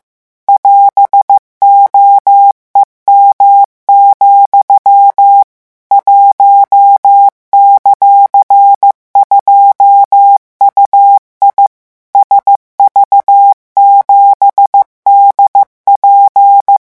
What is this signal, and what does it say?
Morse code, heard as a keyed tone, about 15 wpm, 800 Hz. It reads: <AS>OEM, 1;2UI SV7DP
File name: morse mp3 / revision17-vitesse 14 mots